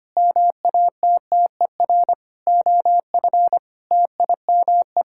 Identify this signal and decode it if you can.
Morse code, heard as a keyed tone, about 25 wpm, 700 Hz. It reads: MATTER OF TIME
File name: Sets of 9 Words - 25wpm